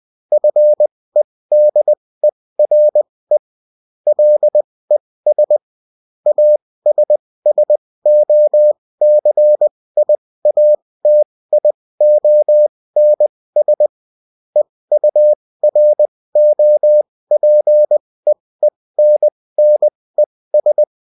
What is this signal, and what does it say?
Morse code, heard as a keyed tone, about 20 wpm, spaced as for 16 wpm, 600 Hz. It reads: FEDERE LES ASSOCIATIONS EUROPEENNES
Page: bulletin-uft-sous-forme-texte-et-en-cw